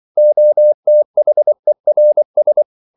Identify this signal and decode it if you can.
Morse code, heard as a keyed tone, about 24 wpm, 600 Hz. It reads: OTHERS